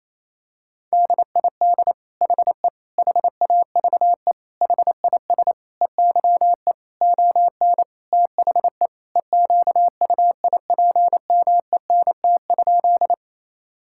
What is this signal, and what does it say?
Morse code, heard as a keyed tone, about 28 wpm, 700 Hz. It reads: DID HE HAVE HIS EYE ON THE EQUIPMENT?
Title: Sentences from Top 1000 Words - Review Part 5 - 28wpm